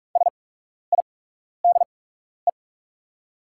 Morse code, heard as a keyed tone, about 44 wpm, spaced as for 12 wpm, 700 Hz. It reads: SIDE
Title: 8x Character Spacing - Top 100 Words - 45wpm